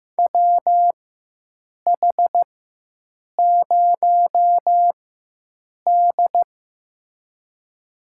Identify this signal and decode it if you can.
Morse code, heard as a keyed tone, about 15 wpm, spaced as for 7 wpm, 700 Hz. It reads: WH0D
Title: USA Call Signs - 4 Characters - 4x Character Spacing - 15wpm